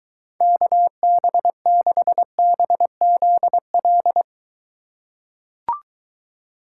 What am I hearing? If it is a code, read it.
Morse code, heard as a keyed tone, about 23 wpm, 700 Hz. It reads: KB6BZL